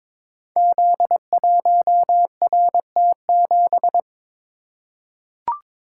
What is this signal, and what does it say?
Morse code, heard as a keyed tone, about 22 wpm, 700 Hz. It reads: Z1RT7
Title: Code Groups - 5 Letter:Numbers - 22wpm